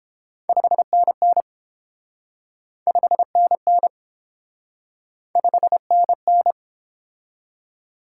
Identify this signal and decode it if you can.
Morse code, heard as a keyed tone, about 27 wpm, 700 Hz. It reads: 5NN 5NN 5NN
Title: Speed-Racing Top 90 QSO Elements 22wpm